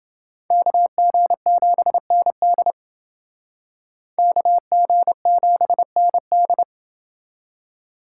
Morse code, heard as a keyed tone, about 28 wpm, 700 Hz. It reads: KG7ND KG7ND